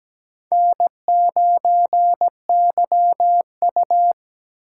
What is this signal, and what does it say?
Morse code, heard as a keyed tone, about 20 wpm, 700 Hz. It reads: N9YU
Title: Speed Racing - USA Call Signs 17wpm